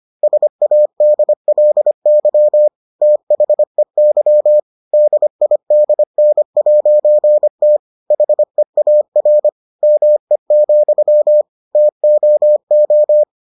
Morse code, heard as a keyed tone, about 25 wpm, 600 Hz. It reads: SADLY THEY DIDN'T HEAR ME, TOO